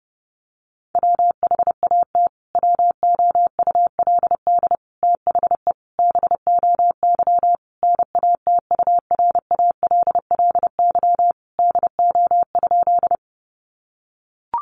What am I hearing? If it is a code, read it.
Morse code, heard as a keyed tone, about 30 wpm, 700 Hz. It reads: WHAT WOULD THE BOY NATURALLY DO?
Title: Sentences from Top 1300 Words 30wpm